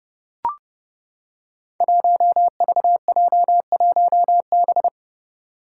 Morse code, heard as a keyed tone, about 30 wpm, 700 Hz. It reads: 1VJ1B
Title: Five Challenging Letter-Number Combinations 30wpm